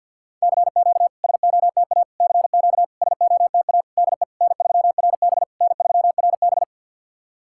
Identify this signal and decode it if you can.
Morse code, heard as a keyed tone, about 50 wpm, 700 Hz. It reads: CQ SOTA CQ SOTA DE N3RB N3RB
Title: SOTA QSO Practice 50wpm